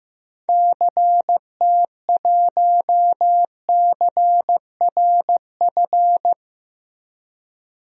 Morse code, heard as a keyed tone, about 15 wpm, 700 Hz. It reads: CT1CRF
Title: DXCC-100 Call Signs - 6 Characters - Repeated 2x - 15wpm